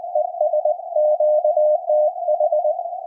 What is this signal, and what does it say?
Morse code, heard as a keyed tone, about 20 wpm, 610 Hz. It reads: ES QTH
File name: attachments / CW